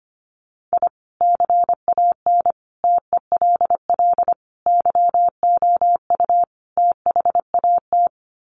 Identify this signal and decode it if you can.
Morse code, heard as a keyed tone, about 25 wpm, 700 Hz. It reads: I CAN TELL YOU THAT
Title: Sets of 5 Words 25wpm